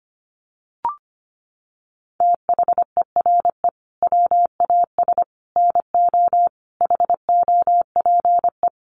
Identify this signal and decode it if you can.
Morse code, heard as a keyed tone, about 25 wpm, 700 Hz. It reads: THERE WAS NO HOPE
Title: Sentences from Top 1000 Words 25wpm